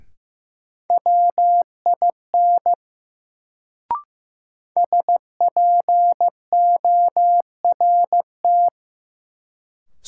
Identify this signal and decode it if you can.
Morse code, heard as a keyed tone, about 16 wpm, 700 Hz. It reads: WIN SPORT
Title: Sentences from Top 1500 Words - New Word Drill Alt - 15wpm